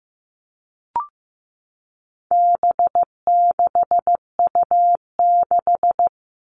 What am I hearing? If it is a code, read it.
Morse code, heard as a keyed tone, about 15 wpm, 700 Hz. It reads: B6U6
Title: All Cut Numbers - 4-Digit 15wpm